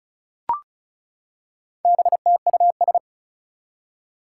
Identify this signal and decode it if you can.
Morse code, heard as a keyed tone, about 36 wpm, 700 Hz. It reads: BTUS
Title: Four Challenging Letter-Number Combinations 35wpm